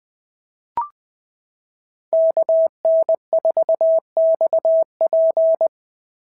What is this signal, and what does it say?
Morse code, heard as a keyed tone, about 19 wpm, 650 Hz. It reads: KN4XP
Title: POTA QSO Practice - 20wpm